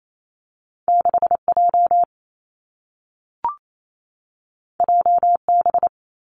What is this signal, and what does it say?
Morse code, heard as a keyed tone, about 28 wpm, 700 Hz. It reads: 6 J J B
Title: Two Challenging Letter-Number Combinations 28wpm